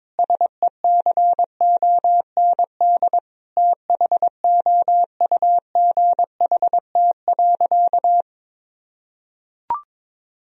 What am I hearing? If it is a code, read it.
Morse code, heard as a keyed tone, about 22 wpm, 700 Hz. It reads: SECOND THOUGHT.
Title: Sentences from Top 1000 Words - Review Part 2 - 22wpm